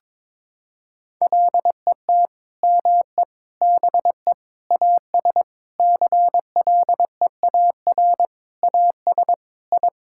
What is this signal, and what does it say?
Morse code, heard as a keyed tone, about 22 wpm, 700 Hz. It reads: LET ME BE AS CLEAR AS I
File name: Sentences with 9-Word Phrases - 22wpm